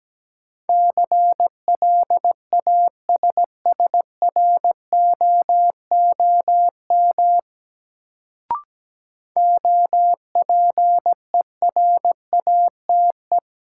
Morse code, heard as a keyed tone, about 17 wpm, 700 Hz. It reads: CLASSROOM OPERATE
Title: Sentences from Top 1500 Words - New Word Drill - 17wpm